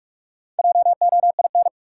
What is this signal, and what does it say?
Morse code, heard as a keyed tone, about 45 wpm, 700 Hz. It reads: JOIN
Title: Sentences from Top 1000 Words - Review Part 6 - New Word Drill - 45wpm